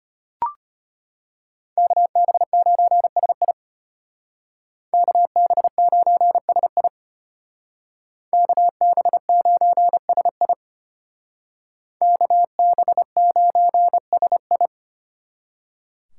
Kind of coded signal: Morse code

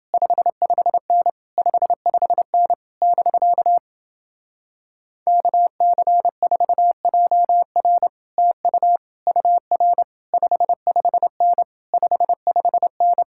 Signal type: Morse code

